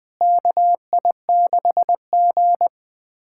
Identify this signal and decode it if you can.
Morse code, heard as a keyed tone, about 20 wpm, 700 Hz. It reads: KI6G
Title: USA Callsign DE Callsign - 4 Characters - 20wpm